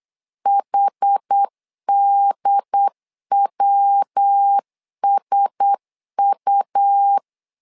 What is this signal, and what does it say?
Morse code, heard as a keyed tone, about 8 wpm, 800 Hz. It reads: HDWSU
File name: LEZIONI / 4 Q1S 8wpm 2710